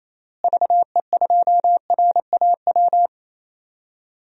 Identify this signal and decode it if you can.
Morse code, heard as a keyed tone, about 28 wpm, 700 Hz. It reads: VE2RAW
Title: DXCC-100 Call Signs - 6 Characters - Repeated 3x - 28wpm